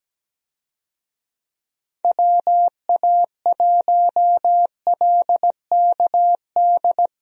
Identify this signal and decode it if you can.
Morse code, heard as a keyed tone, about 17 wpm, 700 Hz. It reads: WA1LKD
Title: USA Call Signs - 6 Characters - Repeated 3x - 17wpm